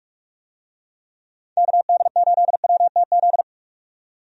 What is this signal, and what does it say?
Morse code, heard as a keyed tone, about 46 wpm, 700 Hz. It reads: KD8WTZ
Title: USA Call Signs - 6 Characters - 45wpm